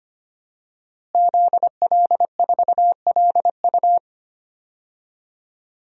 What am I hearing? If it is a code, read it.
Morse code, heard as a keyed tone, about 25 wpm, 700 Hz. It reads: ZL4LU
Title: DXCC-100 Call Signs - 5 Characters - Repeated 3x - 25wpm